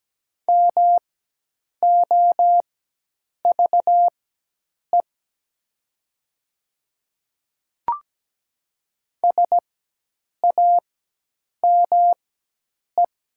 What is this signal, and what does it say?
Morse code, heard as a keyed tone, about 17 wpm, spaced as for 8 wpm, 700 Hz. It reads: MOVE SAME
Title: 4x Character Spacing - Top 200 Words - 17wpm